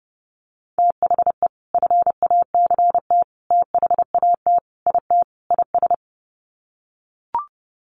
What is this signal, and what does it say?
Morse code, heard as a keyed tone, about 30 wpm, 700 Hz. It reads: THE FACT THAT IT IS
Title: Sets of 5 Words Encore2 30wpm